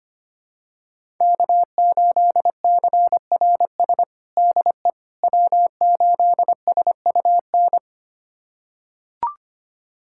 Morse code, heard as a keyed tone, about 25 wpm, 700 Hz. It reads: K8CRS DE W8SUN
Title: USA Callsign DE Callsign - 5 Characters - 25wpm